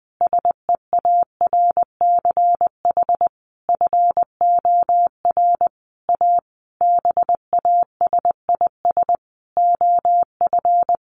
Morse code, heard as a keyed tone, about 19 wpm, 700 Hz. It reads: SEARCH FOR A BASIS OF
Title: Sentences from Top 1500 Words 20wpm